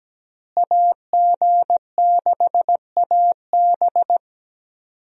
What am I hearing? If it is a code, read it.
Morse code, heard as a keyed tone, about 17 wpm, 700 Hz. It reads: AG6AB
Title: USA Call Signs - 5 Characters - Repeated 3x - 17wpm